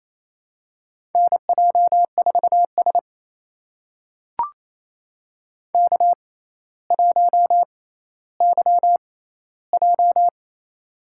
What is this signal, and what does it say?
Morse code, heard as a keyed tone, about 27 wpm, spaced as for 14 wpm, 700 Hz. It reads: NJ4S K1YJ
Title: USA Call Signs - 4 Characters - 6x Character Spacing - 28wpm